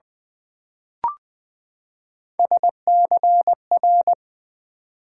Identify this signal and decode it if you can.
Morse code, heard as a keyed tone, about 20 wpm, 700 Hz. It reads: SCR
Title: Build A Word - 3 Letters At A Time - 9-Letter Words - 20wpm